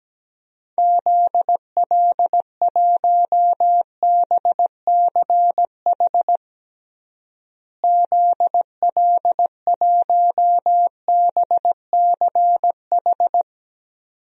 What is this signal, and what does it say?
Morse code, heard as a keyed tone, about 17 wpm, 700 Hz. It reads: ZL1BCH ZL1BCH
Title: DXCC-100 Call Signs - 6 Characters - Repeated 2x - 17wpm